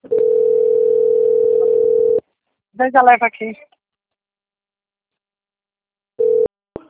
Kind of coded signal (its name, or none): telephone call progress tone